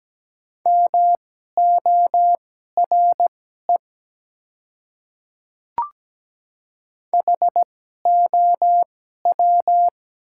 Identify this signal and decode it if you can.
Morse code, heard as a keyed tone, about 17 wpm, spaced as for 12 wpm, 700 Hz. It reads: MORE HOW
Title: 2x Character Spacing - Top 100 Words - 17wpm